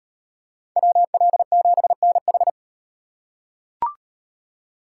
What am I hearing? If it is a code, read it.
Morse code, heard as a keyed tone, about 38 wpm, 700 Hz. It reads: WL7NH